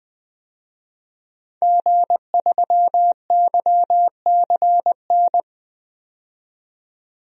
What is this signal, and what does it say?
Morse code, heard as a keyed tone, about 20 wpm, 700 Hz. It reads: G3YCN
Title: DXCC-100 Call Signs - 5 Characters - 20wpm